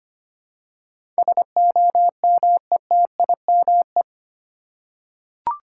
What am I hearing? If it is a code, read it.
Morse code, heard as a keyed tone, about 25 wpm, 700 Hz. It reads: SOMETIME
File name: Eight Letter Words - 25wpm